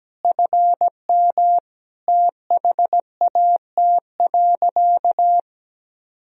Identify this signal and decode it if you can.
Morse code, heard as a keyed tone, about 17 wpm, 700 Hz. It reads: FM THAT.
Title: Sentences from Top 300 Words and Abbreviations 17wpm